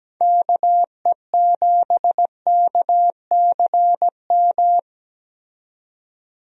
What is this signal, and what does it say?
Morse code, heard as a keyed tone, about 17 wpm, 700 Hz. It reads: KE7KCM
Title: USA Call Signs - 6 Characters - 17wpm